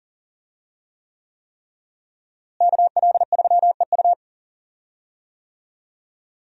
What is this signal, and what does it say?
Morse code, heard as a keyed tone, about 38 wpm, 700 Hz. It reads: KL3EU
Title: USA Call Signs - 5 Characters - Repeated 3x - 40wpm